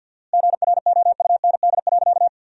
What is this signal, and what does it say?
Morse code, heard as a keyed tone, about 53 wpm, 700 Hz. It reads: GROUND.
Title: 4x Word-Spacing - Sentences from Top 500 Words - 50wpm